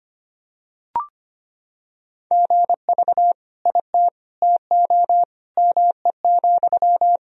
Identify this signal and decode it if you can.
Morse code, heard as a keyed tone, about 25 wpm, 700 Hz. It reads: GV IT TO ME,
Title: Sentences from Top 200 Words and Abbreviations 25wpm